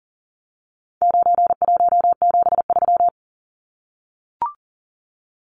Morse code, heard as a keyed tone, about 40 wpm, 700 Hz. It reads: 9173
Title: Numbers - 4-Digit 40wpm